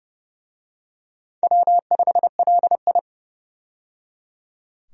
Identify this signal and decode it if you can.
Morse code, heard as a keyed tone, about 29 wpm, 700 Hz. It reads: W5LI
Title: USA Call Signs - 4 Characters - 30wpm